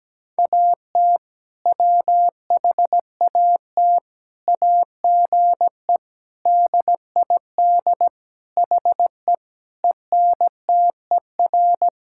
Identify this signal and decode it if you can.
Morse code, heard as a keyed tone, about 17 wpm, 700 Hz. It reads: AT WHAT AGE DID HE ENTER
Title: Sentences from Top 1000 Words - Review Part 5 - 17wpm